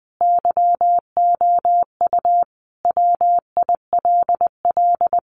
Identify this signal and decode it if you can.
Morse code, heard as a keyed tone, about 20 wpm, 700 Hz. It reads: YOU WILL
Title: Sentences from Top 600 Words 20wpm